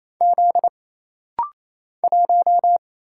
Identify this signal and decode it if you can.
Morse code, heard as a keyed tone, about 28 wpm, 700 Hz. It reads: Z 1
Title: Single Letter-Number - Rapid-Fire 28wpm